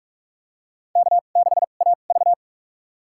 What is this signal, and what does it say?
Morse code, heard as a keyed tone, about 46 wpm, spaced as for 33 wpm, 700 Hz. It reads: K6AV
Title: USA Call Signs - 4 Characters - 2x Character Spacing - 45wpm